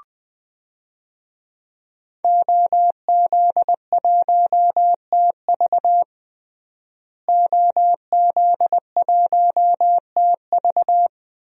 Morse code, heard as a keyed tone, about 20 wpm, 700 Hz. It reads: OZ1TV OZ1TV
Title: DXCC-100 Call Signs - 5 Characters - Repeated 3x - 20wpm